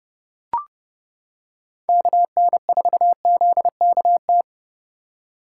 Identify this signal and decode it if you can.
Morse code, heard as a keyed tone, about 30 wpm, 700 Hz. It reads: KN4ZKT
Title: K1USN SST QSO Practice - 30wpm